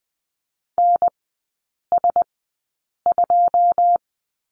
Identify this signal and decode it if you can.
Morse code, heard as a keyed tone, about 20 wpm, spaced as for 8 wpm, 700 Hz. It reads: NS2
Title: 081 Lesson 14e - Number 2 - Three Characters 20wpm